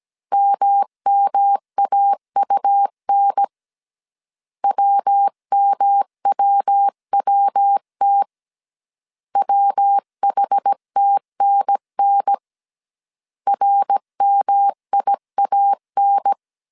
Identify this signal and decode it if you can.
Morse code, heard as a keyed tone, about 17 wpm, 800 Hz. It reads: MMAUN WMWWT WHTNN RMIAN
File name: CW / DET / 1 3 1500C S A 16wpm 2141